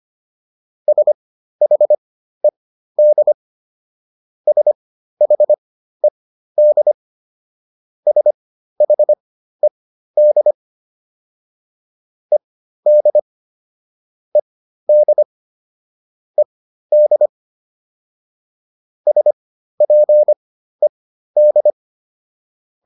Morse code, heard as a keyed tone, about 25 wpm, spaced as for 13 wpm, 600 Hz. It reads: SHED SHED SHED ED ED ED SPED